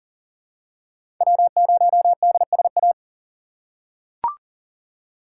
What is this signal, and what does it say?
Morse code, heard as a keyed tone, about 38 wpm, 700 Hz. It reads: W0DSA
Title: USA Call Signs - 4x Character Spacing - 40wpm